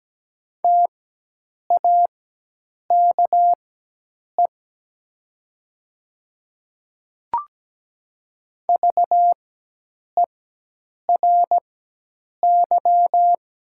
Morse code, heard as a keyed tone, about 17 wpm, spaced as for 8 wpm, 690 Hz. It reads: TAKE VERY
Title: 4x Character Spacing - Top 200 Words - 17wpm